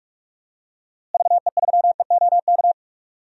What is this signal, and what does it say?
Morse code, heard as a keyed tone, about 46 wpm, 700 Hz. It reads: VE3EOK
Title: DXCC-100 Call Signs - 6 Characters - Repeated 3x - 45wpm